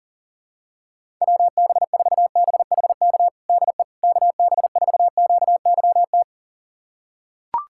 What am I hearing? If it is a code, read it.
Morse code, heard as a keyed tone, about 40 wpm, 700 Hz. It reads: WB4BHK DE KB4QYT
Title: USA Callsign DE Callsign - 6 Characters - 40wpm